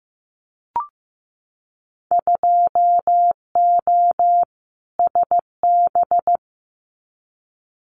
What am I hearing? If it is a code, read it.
Morse code, heard as a keyed tone, about 15 wpm, 700 Hz. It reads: 2O SB
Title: Winter Field Day Exchanges 15wpm